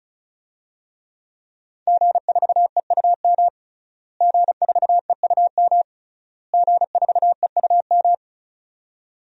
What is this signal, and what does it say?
Morse code, heard as a keyed tone, about 36 wpm, 700 Hz. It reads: G4EUM G4EUM G4EUM
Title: DXCC-100 Call Signs - 5 Characters - Repeated 3x - 35wpm